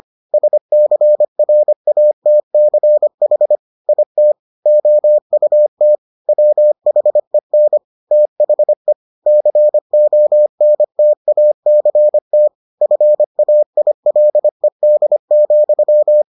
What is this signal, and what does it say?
Morse code, heard as a keyed tone, about 25 wpm, 600 Hz. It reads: SCRATCH IT OUT WHEN THE CONTACT FAILED,